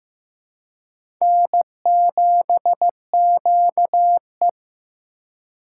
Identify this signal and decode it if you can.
Morse code, heard as a keyed tone, about 15 wpm, 700 Hz. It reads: N7QE
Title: USA Call Signs - 4 Characters - 4x Character Spacing - 15wpm